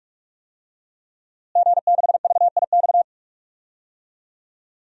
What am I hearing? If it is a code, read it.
Morse code, heard as a keyed tone, about 46 wpm, 700 Hz. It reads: G6VIX
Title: DXCC-100 Call Signs - 5 Characters - Repeated 2x - 45wpm